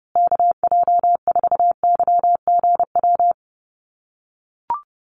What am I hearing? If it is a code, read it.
Morse code, heard as a keyed tone, about 30 wpm, 700 Hz. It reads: KJ4YGW
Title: USA Call Signs Repeated 3x 30wpm